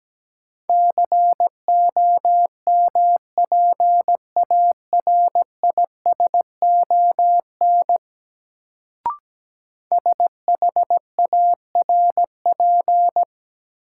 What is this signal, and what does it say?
Morse code, heard as a keyed tone, about 17 wpm, 700 Hz. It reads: COMPARISON SHARP
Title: Sentences from Top 1600 Words - New Word Drill - 17wpm